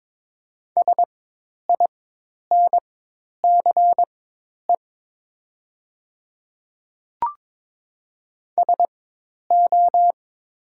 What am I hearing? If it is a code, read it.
Morse code, heard as a keyed tone, about 22 wpm, spaced as for 10 wpm, 700 Hz. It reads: SINCE SO
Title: 4x Character Spacing - Top 300 Words - 22wpm